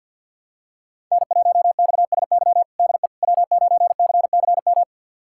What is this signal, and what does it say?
Morse code, heard as a keyed tone, about 45 wpm, 700 Hz. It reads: N1XSY DE W9CCM
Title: USA Callsign DE Callsign - 5 Characters - 50wpm